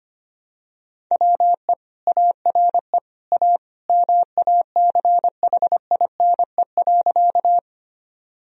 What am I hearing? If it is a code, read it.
Morse code, heard as a keyed tone, about 25 wpm, 700 Hz. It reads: WE ARE A MACHINE.
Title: Sentences from Top 1000 Words - Review Part 2 - 25wpm